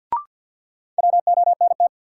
Morse code, heard as a keyed tone, about 53 wpm, 700 Hz. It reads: WONT
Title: Sentences from Top 1000 Words - Review Part 6 - New Word Drill - 50wpm